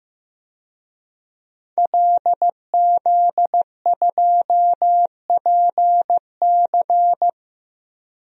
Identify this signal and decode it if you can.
Morse code, heard as a keyed tone, about 15 wpm, 700 Hz. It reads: LZ2PC
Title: DXCC-100 Call Signs - 5 Characters - 15wpm